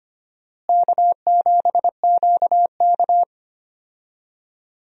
Morse code, heard as a keyed tone, about 25 wpm, 700 Hz. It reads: K7QK
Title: USA Call Signs - 4 Characters - Repeated 2x - 25wpm